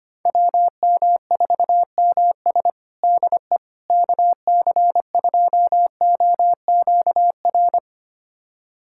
Morse code, heard as a keyed tone, about 25 wpm, 700 Hz. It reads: WM4MS DE KC2OQR
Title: USA Callsign DE Callsign 25wpm